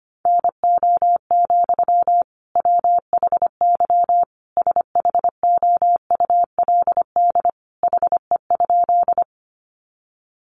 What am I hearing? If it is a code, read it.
Morse code, heard as a keyed tone, about 25 wpm, 700 Hz. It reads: NO, WHY SHOULD HE?
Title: Sentences from Top 300 Words 25wpm